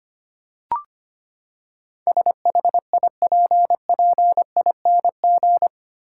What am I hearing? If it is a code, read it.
Morse code, heard as a keyed tone, about 25 wpm, 700 Hz. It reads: SHIPPING
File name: Eight Letter Words - 25wpm